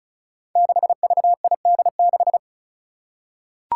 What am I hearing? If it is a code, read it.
Morse code, heard as a keyed tone, about 36 wpm, 700 Hz. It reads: 6VID6
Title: Five Challenging Letter-Number Combinations 35wpm